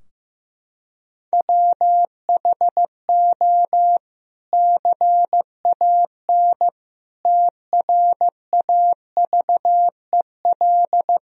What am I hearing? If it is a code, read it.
Morse code, heard as a keyed tone, about 15 wpm, 700 Hz. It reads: WHO CAN TRAVEL
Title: Sentences from Top 1000 Words - Review Part 3 - 15wpm